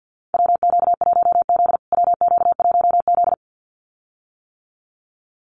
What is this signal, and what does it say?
Morse code, heard as a keyed tone, about 50 wpm, 700 Hz. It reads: WQ1Z WQ1Z
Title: USA Call Signs Repeated 2x 50wpm